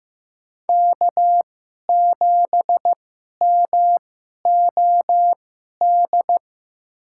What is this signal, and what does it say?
Morse code, heard as a keyed tone, about 15 wpm, spaced as for 11 wpm, 700 Hz. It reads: K7MOD
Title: USA Call Signs - 5 Characters - 2x Character Spacing - 15wpm